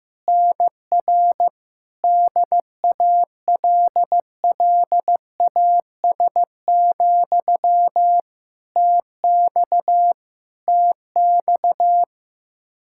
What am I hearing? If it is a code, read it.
Morse code, heard as a keyed tone, about 15 wpm, 700 Hz. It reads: NR DALLAS, TX TX
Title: Practice QSO - Exchange 1 - Beginning - 15wpm